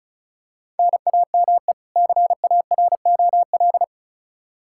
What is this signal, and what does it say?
Morse code, heard as a keyed tone, about 34 wpm, 700 Hz. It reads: NAME CAROL
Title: K1USN SST QSO Practice 35wpm